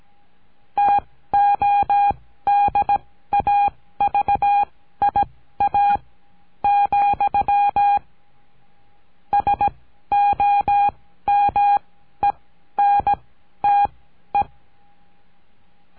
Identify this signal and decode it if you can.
Morse code, heard as a keyed tone, about 17 wpm, spaced as for 12 wpm, 810 Hz. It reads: TODAVIA, SOMENTE
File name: CURSO DE TELEGRAFIA